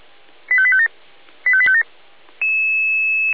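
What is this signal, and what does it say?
Five-tone selective call (ZVEI), heard as quick runs of five tones, tones of about 70 ms. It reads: 85857 85857
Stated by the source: ZVEI five-tone selective call received on SDR.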